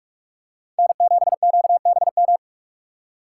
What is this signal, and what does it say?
Morse code, heard as a keyed tone, about 45 wpm, 700 Hz. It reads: N7QBM
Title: USA Call Signs - 5 Characters - 3x Character Spacing - 45wpm